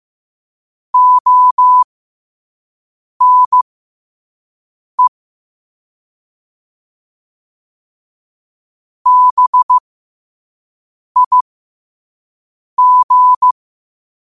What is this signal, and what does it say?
Morse code, heard as a keyed tone, about 15 wpm, spaced as for 5 wpm, 1000 Hz. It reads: ONE BIG
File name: qotdfiles / 5wpm